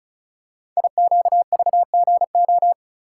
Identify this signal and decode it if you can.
Morse code, heard as a keyed tone, about 36 wpm, 700 Hz. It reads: IQVGO
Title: Code Groups - 5 Letters - 35wpm